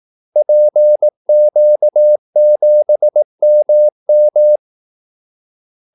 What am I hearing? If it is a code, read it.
Morse code, heard as a keyed tone, about 18 wpm, 600 Hz. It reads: PQ7MM
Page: Kaldesignaler 18 wpm